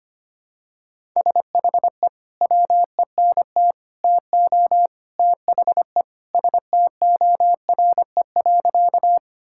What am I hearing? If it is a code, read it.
Morse code, heard as a keyed tone, about 25 wpm, 700 Hz. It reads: SHE WENT TO THE STORE.
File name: practice / 25wpm / Sentences with 3-Word Phrases - 25wpm